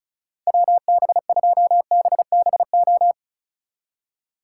Morse code, heard as a keyed tone, about 42 wpm, 700 Hz. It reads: WB2BBO
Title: Speed Racing - USA 2x3 Call Signs 35wpm